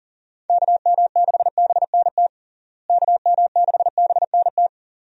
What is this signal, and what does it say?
Morse code, heard as a keyed tone, about 40 wpm, 700 Hz. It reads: KM6BNT KM6BNT
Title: USA Call Signs - 6 Characters - Repeated 2x - 40wpm